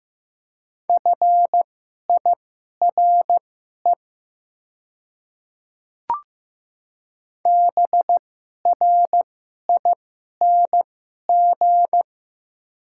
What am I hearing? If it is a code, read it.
Morse code, heard as a keyed tone, about 15 wpm, spaced as for 11 wpm, 700 Hz. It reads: FIRE BRING
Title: 2x Character Spacing - Top 500 Words - 15wpm